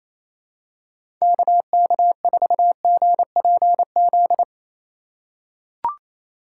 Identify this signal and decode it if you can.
Morse code, heard as a keyed tone, about 28 wpm, 700 Hz. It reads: KK4GPZ
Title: USA Call Signs - 6 Characters - 4x Character Spacing - 28wpm